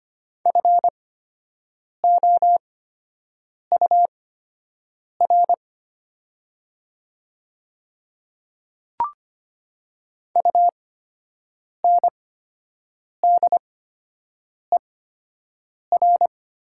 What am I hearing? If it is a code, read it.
Morse code, heard as a keyed tone, about 25 wpm, spaced as for 7 wpm, 700 Hz. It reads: FOUR UNDER
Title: 8x Character Spacing - Top 500 Words - 25wpm